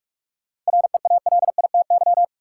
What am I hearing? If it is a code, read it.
Morse code, heard as a keyed tone, about 42 wpm, 700 Hz. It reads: REALITY
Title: Sentences from Top 1200 Words - New Word Drill Alt - 45wpm